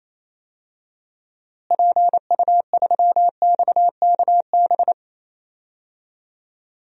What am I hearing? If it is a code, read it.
Morse code, heard as a keyed tone, about 27 wpm, 700 Hz. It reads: PU3XKB
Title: DXCC-100 Call Signs - 6 Characters - Repeated 2x - 28wpm